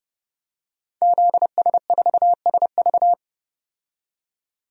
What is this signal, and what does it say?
Morse code, heard as a keyed tone, about 30 wpm, 700 Hz. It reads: ZS4SV
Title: DXCC-100 Call Signs - 5 Characters - 30wpm